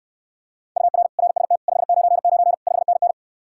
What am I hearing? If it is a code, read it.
Morse code, heard as a keyed tone, about 69 wpm, 700 Hz. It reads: UR RST 599 5NN